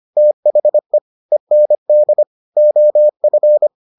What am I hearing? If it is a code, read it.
Morse code, heard as a keyed tone, about 25 wpm, 600 Hz. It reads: THE END OF